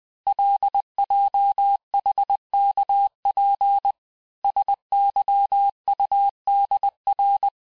Morse code, heard as a keyed tone, about 20 wpm, 800 Hz. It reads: LJHKP SYUDR
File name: cours-858-texte-chiffre-vitesse-19mpm